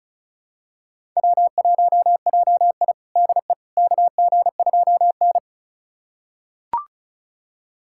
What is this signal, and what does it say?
Morse code, heard as a keyed tone, about 36 wpm, 700 Hz. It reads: W1JI DE KG2N
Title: USA Callsign DE Callsign - 4 Characters - 35wpm